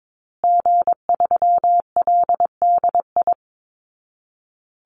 Morse code, heard as a keyed tone, about 22 wpm, 700 Hz. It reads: G3LDI
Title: Call Signs Repeated 2x - 22wpm